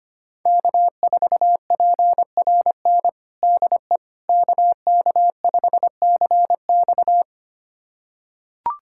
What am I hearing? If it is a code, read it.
Morse code, heard as a keyed tone, about 25 wpm, 700 Hz. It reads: K4PRN DE KK5CX
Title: USA Callsign DE Callsign - 5 Characters - 25wpm